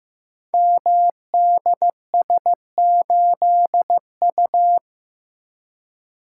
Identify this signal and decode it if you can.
Morse code, heard as a keyed tone, about 15 wpm, 700 Hz. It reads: MDS8U
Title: Code Groups - 5 Letter:Numbers - 15wpm